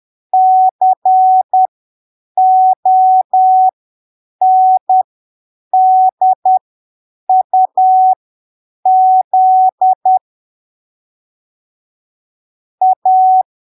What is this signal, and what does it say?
Morse code, heard as a keyed tone, about 10 wpm, spaced as for 7 wpm, 750 Hz. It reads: CONDUZ A
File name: arquivos / texto / texto 05 10ppm